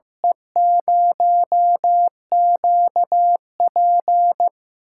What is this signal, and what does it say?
Morse code, heard as a keyed tone, about 15 wpm, 700 Hz. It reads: E0QP
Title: USA Call Signs - 6x Character Spacing - 15wpm